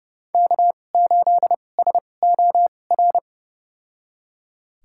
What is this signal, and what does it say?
Morse code, heard as a keyed tone, about 30 wpm, spaced as for 22 wpm, 700 Hz. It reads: K8SOR
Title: USA Call Signs - 5 Characters - 2x Character Spacing - 30wpm